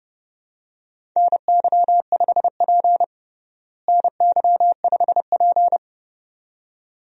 Morse code, heard as a keyed tone, about 30 wpm, 700 Hz. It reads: NY5P NY5P
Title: USA Call Signs - 4 Characters - Repeated 2x - 30wpm